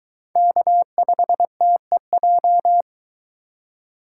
Morse code, heard as a keyed tone, about 23 wpm, 700 Hz. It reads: K5TEJ